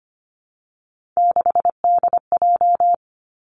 Morse code, heard as a keyed tone, about 25 wpm, 700 Hz. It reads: 6DJ
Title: Three Challenging Letter-Number Combinations 25wpm